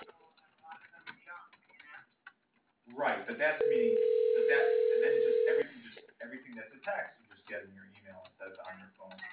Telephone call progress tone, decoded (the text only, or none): ringback tone